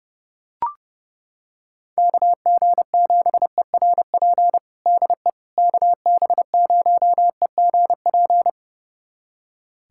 Morse code, heard as a keyed tone, about 30 wpm, 700 Hz. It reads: KG7ERP DE KB0EGP
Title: USA Callsign DE Callsign 30wpm